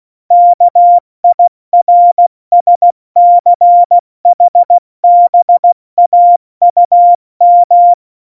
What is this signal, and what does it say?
Morse code, heard as a keyed tone, about 16 wpm, 700 Hz. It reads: KIRSCHBAUM
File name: Morsekurs / Witze / Witz 14. Januar 2020, 70 F80